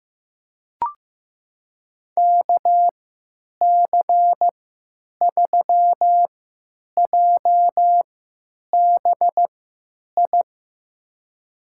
Morse code, heard as a keyed tone, about 15 wpm, spaced as for 9 wpm, 700 Hz. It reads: KC3JBI
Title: USA Call Signs - 3x Character Spacing - 15wpm